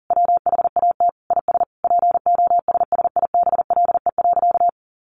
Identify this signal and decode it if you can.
Morse code, heard as a keyed tone, about 40 wpm, 700 Hz. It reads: WHAT IS POSSIBLE.
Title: Sentences from Top 900 Words 40wpm